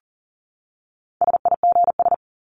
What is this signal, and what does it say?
Morse code, heard as a keyed tone, about 40 wpm, 700 Hz. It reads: SIGS
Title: Abbreviations 40wpm